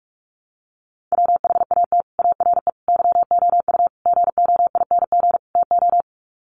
Morse code, heard as a keyed tone, about 44 wpm, 700 Hz. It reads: WHAT ARE YOU GOING TO
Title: Sets of 5 Words 45wpm